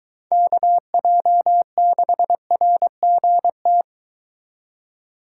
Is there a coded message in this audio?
Yes: Morse code from 0.3 to 3.8 s, heard as a keyed tone, about 24 wpm, 700 Hz. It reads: KJ6RGT